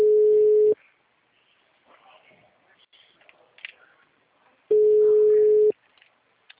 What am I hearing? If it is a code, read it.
Ringback tone (Europe, 425 Hz).